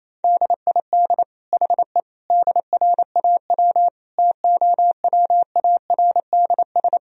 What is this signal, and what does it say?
Morse code, heard as a keyed tone, about 27 wpm, 700 Hz. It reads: DID HE DRAW TOWARDS
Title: Sentences from Top 1000 Words - Review Part 2 - 28wpm